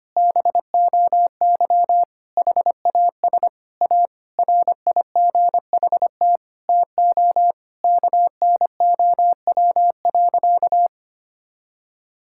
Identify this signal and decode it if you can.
Morse code, heard as a keyed tone, about 25 wpm, 700 Hz. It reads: BOY HAS A RIGHT TO KNOW.
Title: Sentences from Top 1000 Words - Review Part 6 - 25wpm